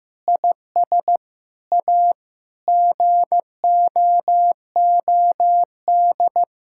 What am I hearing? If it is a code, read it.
Morse code, heard as a keyed tone, about 15 wpm, 700 Hz. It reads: IS A GOOD
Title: Sentences from Top 1000 Words - Review Part 5 - 15wpm